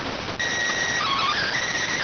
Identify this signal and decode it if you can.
SSTV (slow-scan television), PD 290.
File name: findings / SAGITAL fr15 nb203 3mm.gen-09 rx